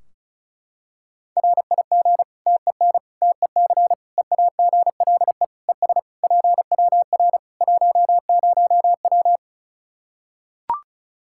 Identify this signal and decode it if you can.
Morse code, heard as a keyed tone, about 36 wpm, 700 Hz. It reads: RIG TEN TEC EAGLE ES PWR 10W
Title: K1USN SST QSO Practice 35wpm